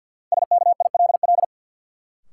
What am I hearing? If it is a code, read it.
Morse code, heard as a keyed tone, about 53 wpm, 700 Hz. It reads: SKILL
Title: Sentences from Top 1200 Words - New Word Drill - 50wpm